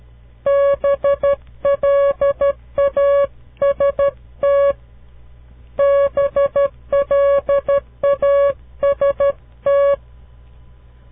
Morse code, heard as a keyed tone, about 12 wpm, 560 Hz. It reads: BLAST BLAST
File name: CW Learning Programs / farnsworth / unit 2